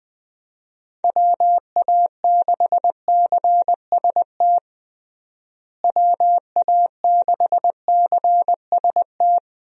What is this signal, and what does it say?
Morse code, heard as a keyed tone, about 20 wpm, 700 Hz. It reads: WA6CST WA6CST
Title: USA Call Signs - 6 Characters - Repeated 3x - 20wpm